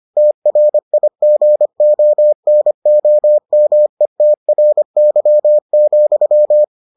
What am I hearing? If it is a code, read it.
Morse code, heard as a keyed tone, about 25 wpm, 600 Hz. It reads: TRIGONOMETRY,